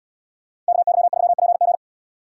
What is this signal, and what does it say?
Morse code, heard as a keyed tone, about 78 wpm, 700 Hz. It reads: KY3PG